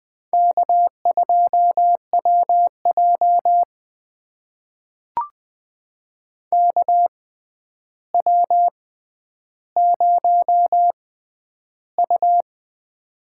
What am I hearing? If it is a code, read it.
Morse code, heard as a keyed tone, about 20 wpm, spaced as for 10 wpm, 700 Hz. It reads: K2WJ KW0U
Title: USA Call Signs - 4 Characters - 6x Character Spacing - 20wpm